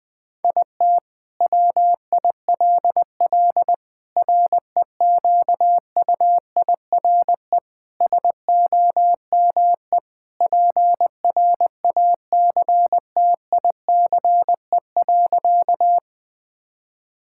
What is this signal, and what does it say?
Morse code, heard as a keyed tone, about 19 wpm, 700 Hz. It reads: IT WILL REQUIRE SOME PRACTICE.
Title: Sentences from Top 1000 Words - Review Part 6 - 20wpm